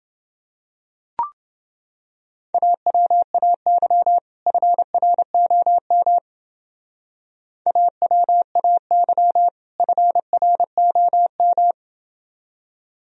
Morse code, heard as a keyed tone, about 28 wpm, 700 Hz. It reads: AWAY FROM AWAY FROM